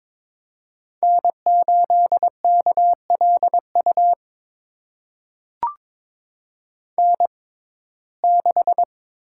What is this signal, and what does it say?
Morse code, heard as a keyed tone, about 22 wpm, spaced as for 16 wpm, 700 Hz. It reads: N8KLU N6
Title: USA Call Signs - 5 Characters - 6x Character Spacing - 22wpm